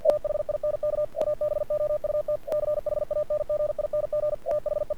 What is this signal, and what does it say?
Morse code, heard as a keyed tone, about 47 wpm, 590 Hz. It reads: THINK ABOUT CHANGING THE
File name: archived / CRSnet